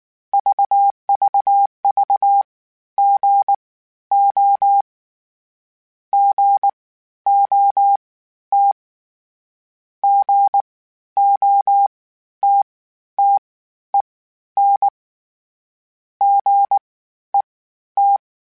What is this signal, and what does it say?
Morse code, heard as a keyed tone, about 19 wpm, spaced as for 12 wpm, 800 Hz. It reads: VVV GO GOT GOTTEN GET
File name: Short Runs / Short Runs-mp3s / Gotten @18